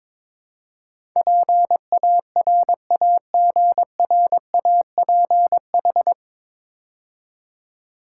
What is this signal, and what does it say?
Morse code, heard as a keyed tone, about 22 wpm, 700 Hz. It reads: PARAGRAPH
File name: 9 Letter Words - First 3 Letters Given - 22wpm